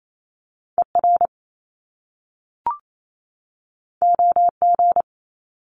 Morse code, heard as a keyed tone, about 28 wpm, 700 Hz. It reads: ER OG
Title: Top Two Letter Combinations in Words 28wpm